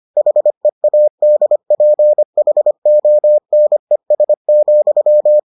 Morse code, heard as a keyed tone, about 25 wpm, 600 Hz. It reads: HEADPHONES,